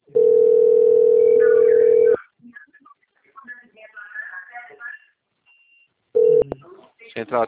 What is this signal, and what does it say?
Ringback tone (North America), 440 + 480 Hz, with speech.
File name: out- - - - - .40836